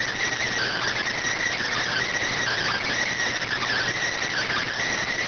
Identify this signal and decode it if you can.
Emergency Alert System (SAME) tones (header bursts), heard as harsh buzzing bursts. It is heard from 0.3 to 4.7 s.